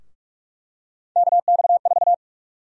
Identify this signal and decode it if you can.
Morse code, heard as a keyed tone, about 41 wpm, 700 Hz. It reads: KX4
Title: USA Call Sign Prefixes - 45wpm